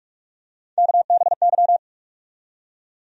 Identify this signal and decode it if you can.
Morse code, heard as a keyed tone, about 41 wpm, 700 Hz. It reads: KBY
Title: USA Call Sign Postfixes - 3 Letters 45wpm